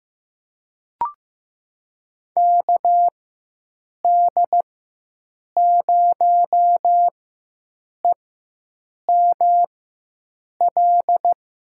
Morse code, heard as a keyed tone, about 15 wpm, spaced as for 7 wpm, 700 Hz. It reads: KD0EML
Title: USA Call Signs - 4x Character Spacing - 15wpm